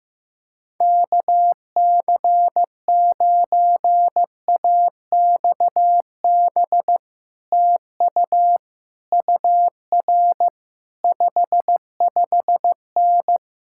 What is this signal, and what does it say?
Morse code, heard as a keyed tone, about 15 wpm, 700 Hz. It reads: KC9AXB TU UR 55N
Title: Speed-Racing Common QSO Phrases Encore - 15wpm